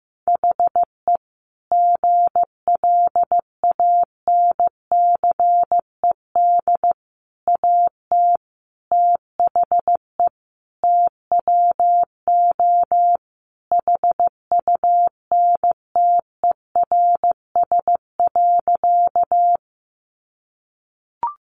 Morse code, heard as a keyed tone, about 15 wpm, 700 Hz. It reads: HE GLANCED AT THE TWO HUNTERS.
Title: A Sound Of Thunder 15wpm